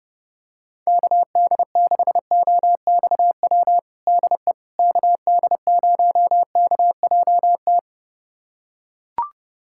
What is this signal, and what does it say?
Morse code, heard as a keyed tone, about 30 wpm, 700 Hz. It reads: KD6OXW DE KD0KJT
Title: USA Callsign DE Callsign - 6 Characters - 30wpm